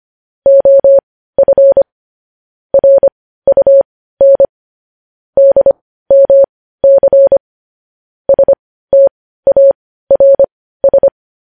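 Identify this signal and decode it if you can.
Morse code, heard as a keyed tone, about 25 wpm, spaced as for 15 wpm, 550 Hz. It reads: OF RUN DMC STARS
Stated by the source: Morse news headlines sent at 15 wpm.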